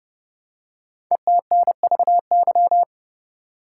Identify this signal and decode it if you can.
Morse code, heard as a keyed tone, about 30 wpm, 700 Hz. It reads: ETNVY
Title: Code Groups - 5 Letters - 30wpm